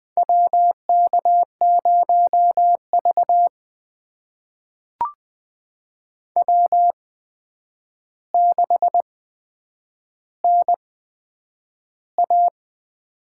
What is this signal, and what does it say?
Morse code, heard as a keyed tone, about 20 wpm, spaced as for 9 wpm, 700 Hz. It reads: WK0V W6NA
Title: USA Call Signs - 4 Characters - 8x Character Spacing - 20wpm